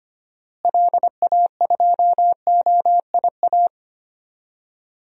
Morse code, heard as a keyed tone, about 25 wpm, 700 Hz. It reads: LA2OIA
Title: DXCC-100 Call Signs - 6 Characters - Repeated 2x - 25wpm